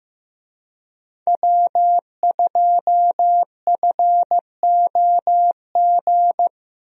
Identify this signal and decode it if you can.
Morse code, heard as a keyed tone, about 15 wpm, 700 Hz. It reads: W2FOG
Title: USA Call Signs - 5 Characters - Repeated 3x - 15wpm